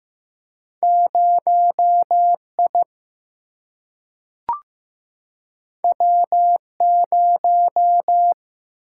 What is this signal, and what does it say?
Morse code, heard as a keyed tone, about 15 wpm, 700 Hz. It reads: 0I W0
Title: Single-Letter Single-Number 15wpm